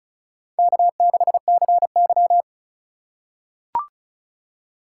Morse code, heard as a keyed tone, about 36 wpm, 700 Hz. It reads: K6CY
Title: USA Call Signs - 4 Characters - 2x Character Spacing - 35wpm